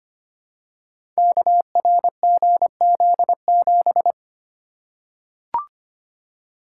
Morse code, heard as a keyed tone, about 25 wpm, 700 Hz. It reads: KRGZ7
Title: Code Groups - 5 Letter:Numbers - 25wpm